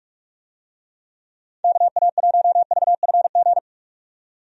Morse code, heard as a keyed tone, about 46 wpm, 700 Hz. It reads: KA1VFG
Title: USA Call Signs - 6 Characters - 45wpm